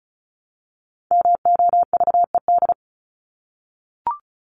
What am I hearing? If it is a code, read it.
Morse code, heard as a keyed tone, about 34 wpm, 700 Hz. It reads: MOVED
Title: ED Suffix - Top 100 Words with Base Words 35wpm